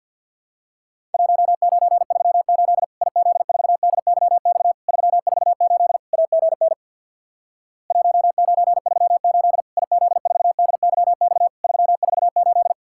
Morse code, heard as a keyed tone, about 52 wpm, 700 Hz. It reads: 1938 IZ4DYX 348 AGN 1938 IZ4DYX 348
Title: Worked All Europe DX Contest - QTC Exchange with Repeats - 50wpm